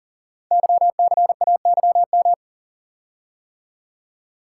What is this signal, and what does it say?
Morse code, heard as a keyed tone, about 38 wpm, 690 Hz. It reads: YCAYM
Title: Code Groups - 5 Letters - 40wpm